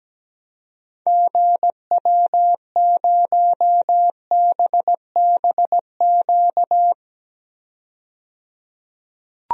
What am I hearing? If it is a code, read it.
Morse code, heard as a keyed tone, about 17 wpm, 700 Hz. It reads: GW0BBQ
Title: DXCC-100 Call Signs - 6 Characters - 17wpm